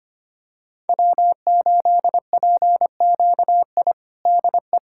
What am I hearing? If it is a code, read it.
Morse code, heard as a keyed tone, about 25 wpm, 700 Hz. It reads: W8PQI DE
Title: USA Callsign DE Callsign - 5 Characters - 25wpm